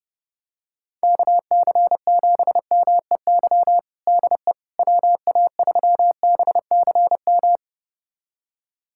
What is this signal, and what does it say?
Morse code, heard as a keyed tone, about 32 wpm, 700 Hz. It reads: KC7MEY DE WA3BCM
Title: USA Callsign DE Callsign 30wpm